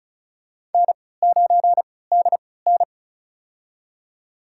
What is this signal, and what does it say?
Morse code, heard as a keyed tone, about 36 wpm, spaced as for 20 wpm, 700 Hz. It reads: N9DN
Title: USA Call Signs - 4 Characters - 3x Character Spacing - 35wpm